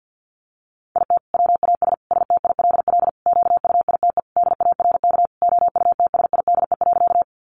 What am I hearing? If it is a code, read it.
Morse code, heard as a keyed tone, about 50 wpm, 700 Hz. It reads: IT WAS STILL QUITE DARK OUTSIDE.
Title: Sentences from Top 900 Words 50wpm